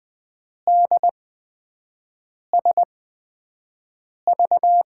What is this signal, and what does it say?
Morse code, heard as a keyed tone, about 20 wpm, spaced as for 5 wpm, 700 Hz. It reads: DSV
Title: USA Call Signs - 5 Characters - 8x Character Spacing - 20wpm